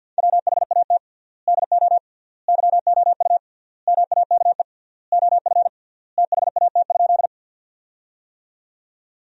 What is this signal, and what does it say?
Morse code, heard as a keyed tone, about 53 wpm, 700 Hz. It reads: WHAT DO YOU MAKE OF THAT?
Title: 3x Word-Spacing - Sentences from Top 100 Words - 50wpm